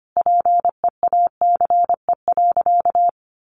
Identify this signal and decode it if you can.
Morse code, heard as a keyed tone, about 25 wpm, 700 Hz. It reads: PEACE.
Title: Sentences from Top 1500 Words 25wpm